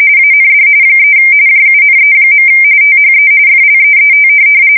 RTTY, about 45.45 baud. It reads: AND 3 JUNE. THE FLEET
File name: fsk170 / USS-Missouri